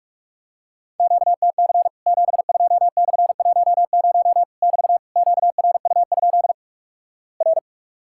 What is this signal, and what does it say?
Morse code, heard as a keyed tone, about 46 wpm, 700 Hz. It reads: QTC 72/10 <BT> QRU? R
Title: Worked All Europe DX Contest - QTC Exchange - 45wpm